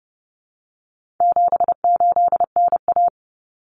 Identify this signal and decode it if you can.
Morse code, heard as a keyed tone, about 30 wpm, 700 Hz. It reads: 78NA
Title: Common Cut Numbers - 4-Digit 30wpm